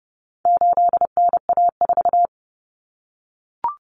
Morse code, heard as a keyed tone, about 30 wpm, 700 Hz. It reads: 8NA4
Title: Common Cut Numbers - 4-Digit 30wpm